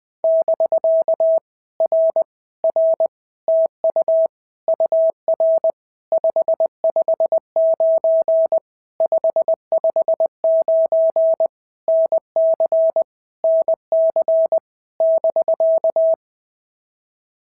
Morse code, heard as a keyed tone, about 20 wpm, 650 Hz. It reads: <BK> R R TU UR 559 559 NC NC <BK>
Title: POTA QSO Practice - 20wpm